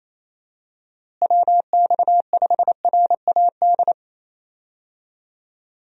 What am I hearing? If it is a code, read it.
Morse code, heard as a keyed tone, about 28 wpm, 700 Hz. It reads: WX5RAD
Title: USA Call Signs - 6 Characters - Repeated 3x - 28wpm